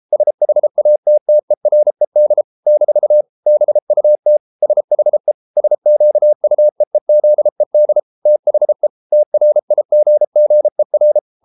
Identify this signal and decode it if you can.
Morse code, heard as a keyed tone, about 33 wpm, 600 Hz. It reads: SHATTERED - BUT SHE SQUEEZED THE TRIGGER